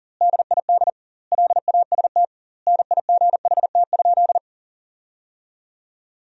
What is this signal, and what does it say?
Morse code, heard as a keyed tone, about 40 wpm, 700 Hz. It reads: DID LAST NIGHT?
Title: 2x Word-Spacing - Sentences from Top 300 Words - 40wpm